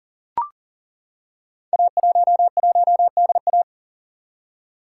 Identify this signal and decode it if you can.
Morse code, heard as a keyed tone, about 38 wpm, 690 Hz. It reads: A11DA
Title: Code Groups - 5 Letter:Numbers - 40wpm